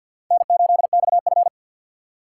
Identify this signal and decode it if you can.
Morse code, heard as a keyed tone, about 62 wpm, 700 Hz. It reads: N8XF
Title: Speed Racing - USA Call Signs 50wpm